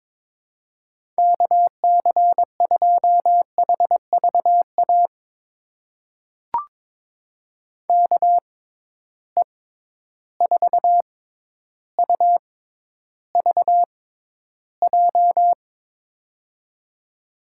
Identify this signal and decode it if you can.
Morse code, heard as a keyed tone, about 21 wpm, spaced as for 11 wpm, 700 Hz. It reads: KC2HVA KE4UVJ
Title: USA Call Signs - 6 Characters - 6x Character Spacing - 22wpm